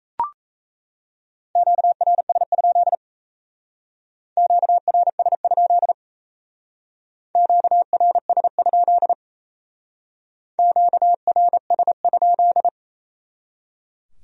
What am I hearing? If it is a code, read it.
Morse code, heard as a keyed tone, about 35 wpm, 700 Hz. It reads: QRS? QRS? QRS? QRS?